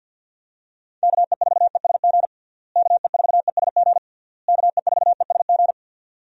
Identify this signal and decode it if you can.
Morse code, heard as a keyed tone, about 53 wpm, 700 Hz. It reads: KE4ESG KE4ESG KE4ESG
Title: USA Call Signs - 6 Characters - Repeated 3x - 50wpm